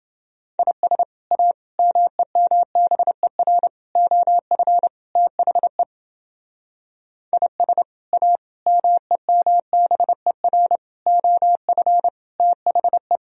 Morse code, heard as a keyed tone, about 29 wpm, 700 Hz. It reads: IS A MEMBER OF THE IS A MEMBER OF THE